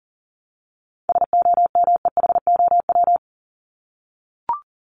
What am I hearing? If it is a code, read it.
Morse code, heard as a keyed tone, about 40 wpm, 700 Hz. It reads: SOMEHOW
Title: Top 1501-1750 Words 40wpm